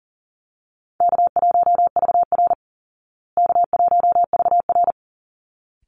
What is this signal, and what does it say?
Morse code, heard as a keyed tone, about 40 wpm, 700 Hz. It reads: K1VR K1VR
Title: Easy Call Signs Repeated 2x - 40wpm